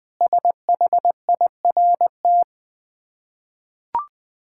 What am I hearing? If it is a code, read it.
Morse code, heard as a keyed tone, about 20 wpm, 700 Hz. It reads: SHIRT
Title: Difficult Words with Hints - 20wpm